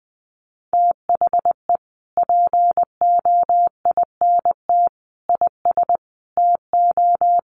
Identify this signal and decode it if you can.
Morse code, heard as a keyed tone, about 20 wpm, 700 Hz. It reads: THE POINT IS TO
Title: Sentences from Top 1300 Words 20wpm